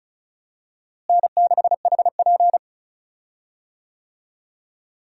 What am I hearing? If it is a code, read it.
Morse code, heard as a keyed tone, about 36 wpm, 700 Hz. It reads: N6HP
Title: USA Call Signs - 4 Characters - Repeated 2x - 35wpm